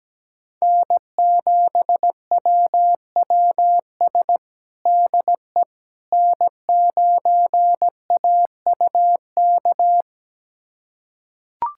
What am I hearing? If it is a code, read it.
Morse code, heard as a keyed tone, about 17 wpm, 700 Hz. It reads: N7WWS DE N9AUK
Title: USA Callsign DE Callsign - 5 Characters - 17wpm